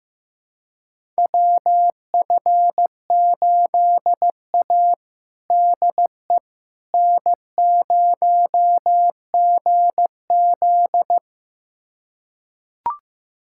Morse code, heard as a keyed tone, about 15 wpm, 700 Hz. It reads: WF8A DE N0GZ
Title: USA Callsign DE Callsign - 4 Characters - 15wpm